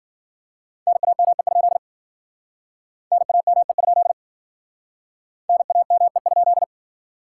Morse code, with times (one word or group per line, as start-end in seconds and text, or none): 0.9-1.8 NAME?
3.1-4.1 NAME?
5.5-6.6 NAME?